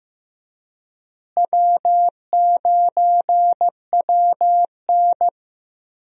Morse code, heard as a keyed tone, about 15 wpm, 700 Hz. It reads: W9WN
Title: USA Call Signs - 4 Characters - Repeated 3x - 15wpm